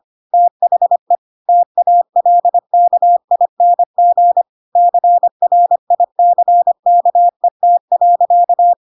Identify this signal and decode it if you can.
Morse code, heard as a keyed tone, about 25 wpm, 700 Hz. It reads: THE TALKING CRICKET.
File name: Pinocchio / Pinocchio ch4 25 25 ik1hge